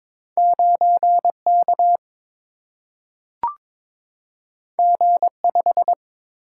Morse code, heard as a keyed tone, about 22 wpm, 700 Hz. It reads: 9 K G 5
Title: Single-Letter Single-Number 22wpm